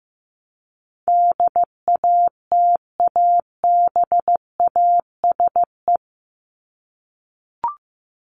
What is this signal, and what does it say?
Morse code, heard as a keyed tone, about 15 wpm, 700 Hz. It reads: DATABASE
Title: Top 1751-2000 Words 15wpm